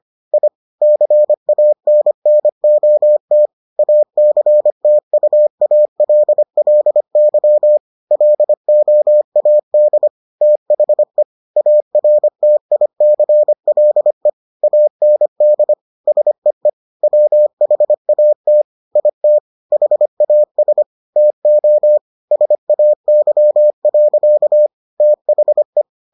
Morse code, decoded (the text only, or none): I CANNOT ACTUALLY LOAD THE ARTICLE AND SEE WHAT IT HAS TO SAY. THE